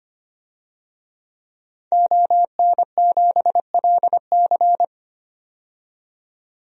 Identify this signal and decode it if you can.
Morse code, heard as a keyed tone, about 25 wpm, 700 Hz. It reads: ON7LC
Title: DXCC-100 Call Signs - 5 Characters - 25wpm